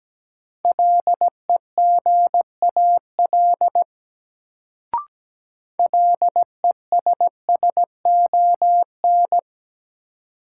Morse code, heard as a keyed tone, about 17 wpm, 700 Hz. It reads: LEGAL LESSON
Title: Sentences from Top 1200 Words - New Word Drill Alt - 17wpm